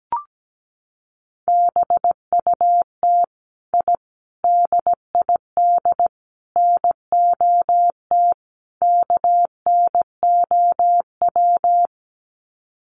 Morse code, heard as a keyed tone, about 17 wpm, 700 Hz. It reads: BUT I DID NOT KNOW
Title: Sets of 5 Words Encore 17wpm